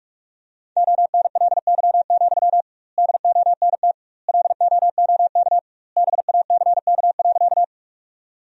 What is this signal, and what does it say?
Morse code, heard as a keyed tone, about 44 wpm, 700 Hz. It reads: ONLY, DONT LOOK BACK.
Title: 2x Word-Spacing - Sentences from Top 300 Words - 45wpm